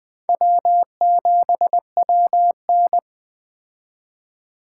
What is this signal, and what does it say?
Morse code, heard as a keyed tone, about 21 wpm, 700 Hz. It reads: W7WN